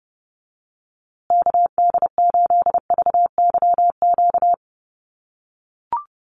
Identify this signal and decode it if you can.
Morse code, heard as a keyed tone, about 30 wpm, 700 Hz. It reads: KD8VYQ
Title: USA Call Signs Repeated 2x 30wpm